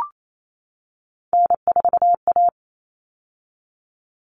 Morse code, heard as a keyed tone, about 28 wpm, 700 Hz. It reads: N4A
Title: All Cut Numbers - 3-Digit 28wpm